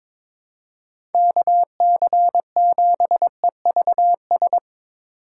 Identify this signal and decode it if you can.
Morse code, heard as a keyed tone, about 22 wpm, 700 Hz. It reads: KC7EVS
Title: USA Call Signs - 6 Characters - 2x Character Spacing - 22wpm